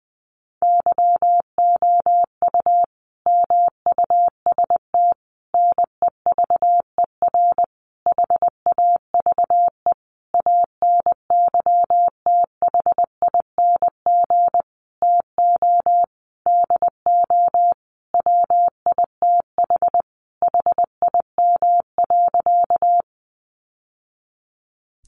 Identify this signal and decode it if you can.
Morse code, heard as a keyed tone, about 20 wpm, 700 Hz. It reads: YOU MUST NEVER HAVE ANYTHING TO DO WITH HIM.
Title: Sentences from Top 1000 Words 20wpm